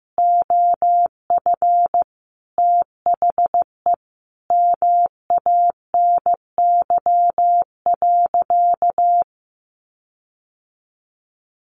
Morse code, heard as a keyed tone, about 14 wpm, 700 Hz. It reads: OF THE MANY.
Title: Sentences from Top 1300 Words 15wpm